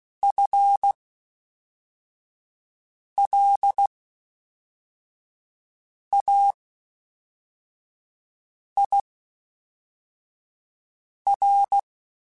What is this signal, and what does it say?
Morse code, heard as a keyed tone, about 16 wpm, spaced as for 4 wpm, 800 Hz. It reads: FLAIR
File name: Spaced / FL Runs sp mp3s / FL Runs sp @5